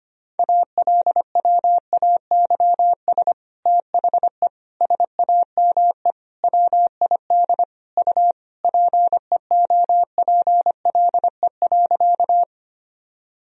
Morse code, heard as a keyed tone, about 25 wpm, 700 Hz. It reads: ALWAYS THE SAME WID U PEOPLE.
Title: Sentences from Top 300 Words and Abbreviations 25wpm